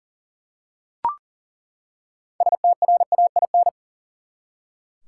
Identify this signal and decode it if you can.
Morse code, heard as a keyed tone, about 38 wpm, 700 Hz. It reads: STRAIN
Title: Six Letter Words - 40wpm